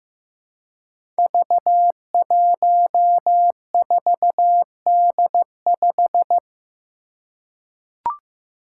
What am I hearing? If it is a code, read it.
Morse code, heard as a keyed tone, about 15 wpm, 700 Hz. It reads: V14D5
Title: Five Challenging Letter-Number Combinations 15wpm